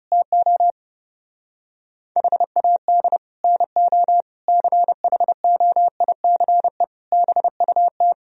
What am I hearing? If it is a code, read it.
Morse code, heard as a keyed tone, about 35 wpm, 700 Hz. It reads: TO HAD NO CHOICE BUT
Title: Speed-Racing - Sets of 5 Words 30wpm